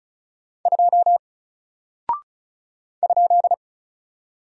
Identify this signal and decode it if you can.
Morse code, heard as a keyed tone, about 34 wpm, 700 Hz. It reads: 2 ?
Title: Single Character (Letter:Number:Punctuation:Prosign) Alt 35wpm